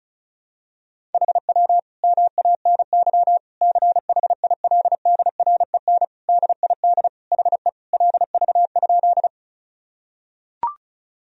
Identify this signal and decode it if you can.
Morse code, heard as a keyed tone, about 36 wpm, 700 Hz. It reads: HW MANY CHILDREN DID HE LV?
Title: Sentences from Top 400 Words and Abbreviations 35wpm